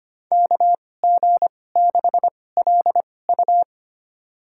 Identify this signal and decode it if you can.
Morse code, heard as a keyed tone, about 25 wpm, spaced as for 18 wpm, 700 Hz. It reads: KG6LU
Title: USA Call Signs - 5 Characters - 2x Character Spacing - 25wpm